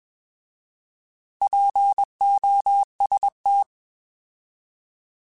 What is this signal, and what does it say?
Morse code, heard as a keyed tone, about 21 wpm, 800 Hz. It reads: POST
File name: Alphabetical / Alphabetical mp3s / Ps @20